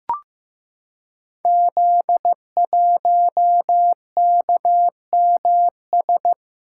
Morse code, heard as a keyed tone, about 15 wpm, 700 Hz. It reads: Z1KMS
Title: Code Groups - 5 Letter:Numbers - 15wpm